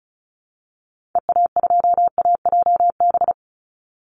Morse code, heard as a keyed tone, about 34 wpm, 700 Hz. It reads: EA2AJB
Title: Call Signs Repeated 2x - 35wpm